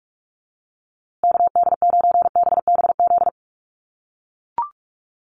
Morse code, heard as a keyed tone, about 44 wpm, 700 Hz. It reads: KD9BBZ
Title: USA Call Signs Repeated 3x 45wpm